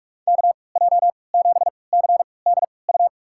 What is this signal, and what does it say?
Morse code, heard as a keyed tone, about 46 wpm, spaced as for 26 wpm, 700 Hz. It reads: KJ7CDU
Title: USA Call Signs - 3x Character Spacing - 45wpm